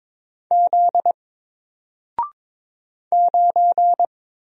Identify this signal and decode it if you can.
Morse code, heard as a keyed tone, about 22 wpm, 700 Hz. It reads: Z 9
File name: Single-Letter-Number-22wpm